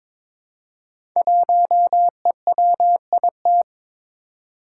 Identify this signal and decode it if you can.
Morse code, heard as a keyed tone, about 22 wpm, 700 Hz. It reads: 1EWIT
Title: Five Challenging Letter-Number Combinations 22wpm